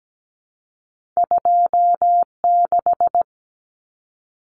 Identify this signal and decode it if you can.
Morse code, heard as a keyed tone, about 17 wpm, 700 Hz. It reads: 26
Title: Common Cut Numbers - 2-Digit 17wpm